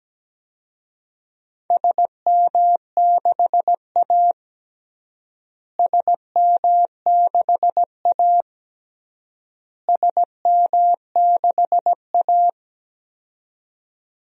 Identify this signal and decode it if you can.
Morse code, heard as a keyed tone, about 17 wpm, 700 Hz. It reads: SM6A SM6A SM6A
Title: DXCC-100 Call Signs - 4 Characters - Repeated 3x - 17wpm